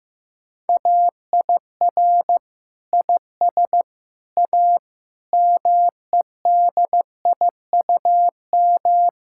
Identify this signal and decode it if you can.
Morse code, heard as a keyed tone, about 14 wpm, 700 Hz. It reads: AIR IS A MEDIUM
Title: Sentences from Top 1000 Words - Review Part 6 - 15wpm